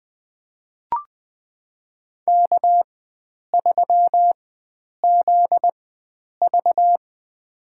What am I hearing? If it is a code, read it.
Morse code, heard as a keyed tone, about 20 wpm, spaced as for 9 wpm, 700 Hz. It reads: K3ZV
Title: USA Call Signs - 4 Characters - 4x Character Spacing - 20wpm